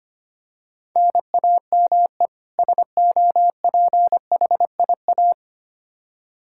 Morse code, heard as a keyed tone, about 25 wpm, 700 Hz. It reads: NAME SOPHIA
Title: Practice QSO - Exchange 1 - Beginning - 25wpm